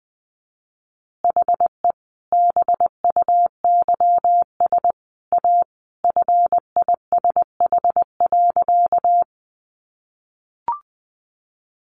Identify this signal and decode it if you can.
Morse code, heard as a keyed tone, about 20 wpm, 700 Hz. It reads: HE BUYS A FISH.
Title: Easy Phrases 20wpm